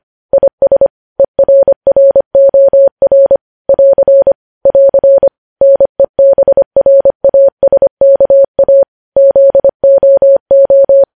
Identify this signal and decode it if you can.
Morse code, heard as a keyed tone, about 25 wpm, 550 Hz. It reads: IS ERROR <AR> <AR> NEBRASKA ZOO
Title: News Headlines at 25